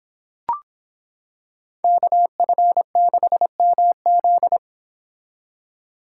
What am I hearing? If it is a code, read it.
Morse code, heard as a keyed tone, about 26 wpm, 700 Hz. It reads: KF6MZ